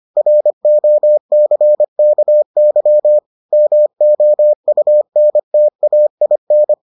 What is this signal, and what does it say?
Morse code, heard as a keyed tone, about 25 wpm, 600 Hz. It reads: ROCKY MOUNTAIN